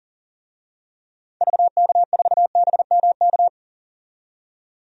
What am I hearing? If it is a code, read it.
Morse code, heard as a keyed tone, about 39 wpm, 700 Hz. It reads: VK4BMK
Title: DXCC-100 Call Signs - 6 Characters - 40wpm